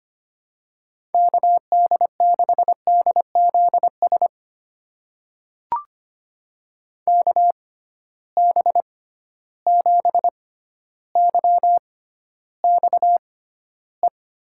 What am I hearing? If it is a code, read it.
Morse code, heard as a keyed tone, about 25 wpm, spaced as for 13 wpm, 700 Hz. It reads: KD6DZS KB7YXE
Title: USA Call Signs - 6 Characters - 6x Character Spacing - 25wpm